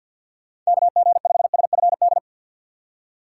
Morse code, heard as a keyed tone, about 50 wpm, 700 Hz. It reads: KG5SFD
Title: USA Call Signs - 6 Characters - 8x Character Spacing - 50wpm